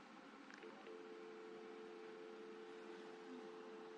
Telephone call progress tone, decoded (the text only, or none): dial tone